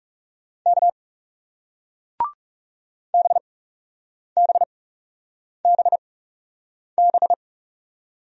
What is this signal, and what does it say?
Morse code, heard as a keyed tone, about 40 wpm, spaced as for 8 wpm, 700 Hz. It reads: K BBBB